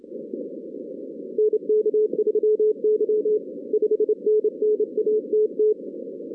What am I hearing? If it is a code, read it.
Morse code, heard as a keyed tone, about 29 wpm, 440 Hz. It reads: NK3Y 5NN ATT
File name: 2016 arrl dx cw / 2016-ARRL-DX-CW-K5TR- - 0-41